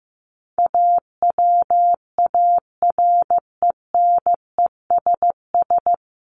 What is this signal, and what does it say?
Morse code, heard as a keyed tone, about 15 wpm, 700 Hz. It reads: AWARENESS
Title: Top 1501-1750 Words 15wpm